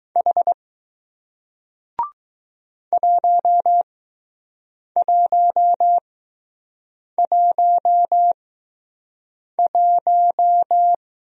Morse code, heard as a keyed tone, about 20 wpm, spaced as for 6 wpm, 700 Hz. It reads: H 1111